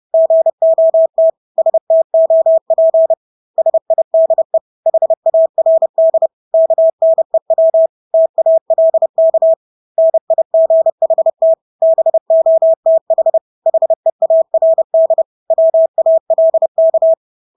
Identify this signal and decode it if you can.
Morse code, heard as a keyed tone, about 30 wpm, 650 Hz. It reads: GOT STOP SIDE HARD KNEW TALK NIGHT BOTH HEARD WALK
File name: Engelske ord 30-30 wpm